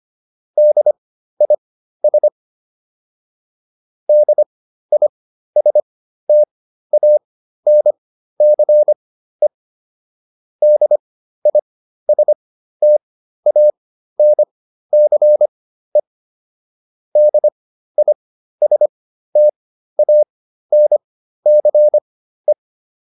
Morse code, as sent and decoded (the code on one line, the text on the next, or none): -.. .. ... / -.. .. ... - .- -. -.-. / -.. .. ... - .- -. -.-. / -.. .. ... - .- -. -.-. .
DIS DISTANCE DISTANCE DISTANCE